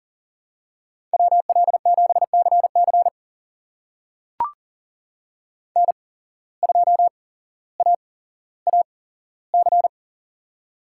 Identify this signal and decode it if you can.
Morse code, heard as a keyed tone, about 40 wpm, spaced as for 17 wpm, 700 Hz. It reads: WL7CC N2AAC
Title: USA Call Signs - 5 Characters - 8x Character Spacing - 40wpm